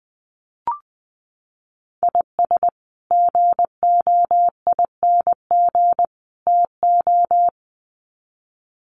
Morse code, heard as a keyed tone, about 20 wpm, 700 Hz. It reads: IS GOING TO
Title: Sets of 3 Words 20wpm